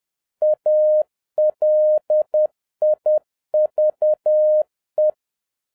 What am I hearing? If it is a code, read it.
Morse code, heard as a keyed tone, about 10 wpm, 610 Hz. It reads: ALIVE
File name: 2011-07-mp3 / n36 THE 3.11 2 10wpm